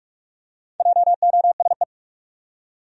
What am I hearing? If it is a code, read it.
Morse code, heard as a keyed tone, about 46 wpm, 700 Hz. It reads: JOSE
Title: USA Common Names - 45wpm